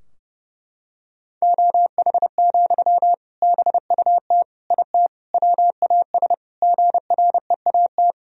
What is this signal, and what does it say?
Morse code, heard as a keyed tone, about 32 wpm, 700 Hz. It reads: OH, BUT IT WAS GREAT
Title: Sentences from Top 1000 Words - Review Part 4 - 30wpm